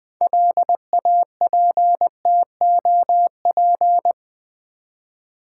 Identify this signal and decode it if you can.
Morse code, heard as a keyed tone, about 20 wpm, 700 Hz. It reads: LAPTOP
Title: 6 Letter Words - First 3 Letters Given - 20wpm